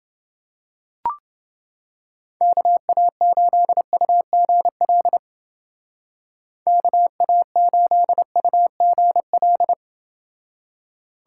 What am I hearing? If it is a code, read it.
Morse code, heard as a keyed tone, about 28 wpm, 700 Hz. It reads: KA8UGL KA8UGL